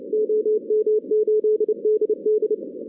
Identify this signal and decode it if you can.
Morse code, heard as a keyed tone, about 28 wpm, 440 Hz. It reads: OM8DD
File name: 2016 arrl dx cw / 2016-ARRL-DX-CW-K5TR- - 0-41